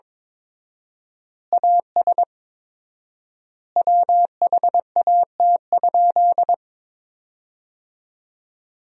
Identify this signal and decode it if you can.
Morse code, heard as a keyed tone, about 22 wpm, 700 Hz. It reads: AS WHAT?
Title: 4x Word-Spacing - Sentences from Top 200 Words - 22wpm